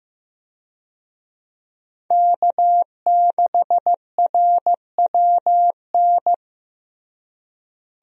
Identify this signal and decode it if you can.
Morse code, heard as a keyed tone, about 15 wpm, 700 Hz. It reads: K6RWN
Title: USA Call Signs - 5 Characters - Repeated 2x - 15wpm